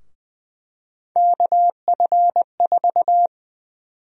Morse code, heard as a keyed tone, about 20 wpm, 700 Hz. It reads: KF4
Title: USA Call Sign Prefixes - 20wpm